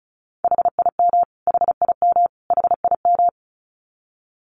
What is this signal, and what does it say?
Morse code, heard as a keyed tone, about 34 wpm, 700 Hz. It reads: HIM HIM HIM
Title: Top 500 Words Repeated 35wpm